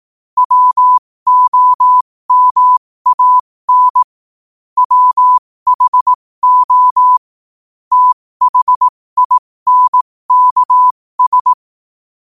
Morse code, heard as a keyed tone, about 18 wpm, spaced as for 15 wpm, 1000 Hz. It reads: WOMAN WHO THINKS
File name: qotdfiles / 15wpm